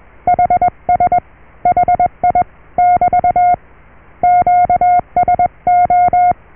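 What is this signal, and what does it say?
Morse code, heard as a keyed tone, about 21 wpm, 720 Hz. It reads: HS HI <BT> QSO
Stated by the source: Morse QSO on 14 MHz with a German station, received at RST 579.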